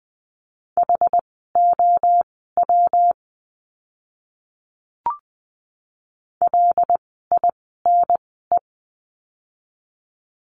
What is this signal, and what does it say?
Morse code, heard as a keyed tone, about 20 wpm, spaced as for 14 wpm, 700 Hz. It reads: HOW LINE
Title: 115 Lesson 19f - Letter w - Words 20wpm